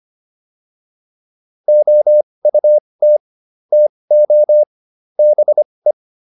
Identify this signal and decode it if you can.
Morse code, heard as a keyed tone, about 25 wpm, spaced as for 20 wpm, 600 Hz. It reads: OUT TO BE